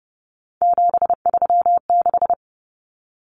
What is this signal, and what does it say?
Morse code, heard as a keyed tone, about 30 wpm, 700 Hz. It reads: 736
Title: Common Cut Numbers - 3-Digit 30wpm